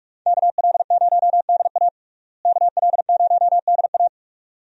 Morse code, heard as a keyed tone, about 46 wpm, 700 Hz. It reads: KL0DA KL0DA
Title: USA Call Signs - 5 Characters - Repeated 2x - 45wpm